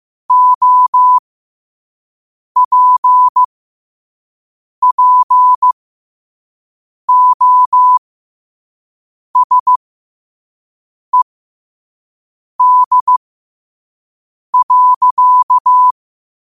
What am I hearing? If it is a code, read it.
Morse code, heard as a keyed tone, about 15 wpm, spaced as for 5 wpm, 1000 Hz. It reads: OPPOSED.
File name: qotdfiles / 5wpm